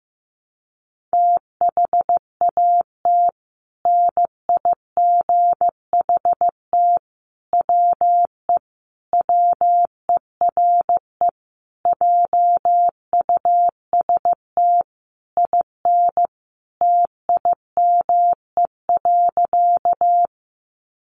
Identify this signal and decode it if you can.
Morse code, heard as a keyed tone, about 15 wpm, 700 Hz. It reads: THAT NIGHT WE WERE JUST IN TIME.
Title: Sentences from Top 300 Words 15wpm